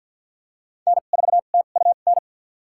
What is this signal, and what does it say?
Morse code, heard as a keyed tone, about 46 wpm, spaced as for 34 wpm, 700 Hz. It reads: N4TUN
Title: USA Call Signs - 5 Characters - 2x Character Spacing - 50wpm